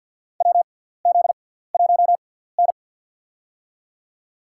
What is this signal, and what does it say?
Morse code, heard as a keyed tone, about 53 wpm, spaced as for 17 wpm, 700 Hz. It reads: WZ1N
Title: USA Call Signs - 4 Characters - 6x Character Spacing - 50wpm